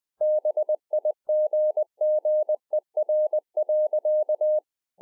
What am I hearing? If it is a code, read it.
Morse code, heard as a keyed tone, about 20 wpm, 610 Hz. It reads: BIGGER.